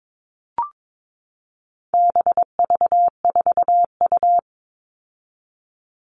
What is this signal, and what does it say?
Morse code, heard as a keyed tone, about 22 wpm, 700 Hz. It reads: BV4U
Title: All Cut Numbers - 4-Digit 22wpm